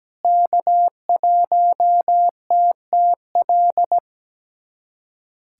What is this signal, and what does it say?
Morse code, heard as a keyed tone, about 17 wpm, 700 Hz. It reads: K1TTL
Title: USA Callsign DE Callsign - 5 Characters - 17wpm